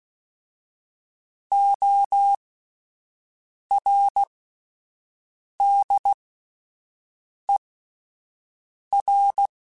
Morse code, heard as a keyed tone, about 16 wpm, spaced as for 5 wpm, 800 Hz. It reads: ORDER
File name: Alphabetical / Alphabetical mp3s / Os @5